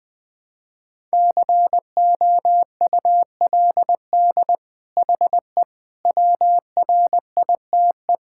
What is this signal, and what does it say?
Morse code, heard as a keyed tone, about 20 wpm, 700 Hz. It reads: COULD HE WRITE
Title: Sentences from Top 1000 Words - Review Part 6 - 20wpm